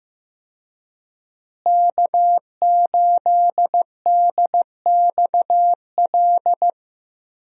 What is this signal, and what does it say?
Morse code, heard as a keyed tone, about 15 wpm, 700 Hz. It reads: K8DXL
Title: USA Call Signs - 5 Characters - Repeated 3x - 15wpm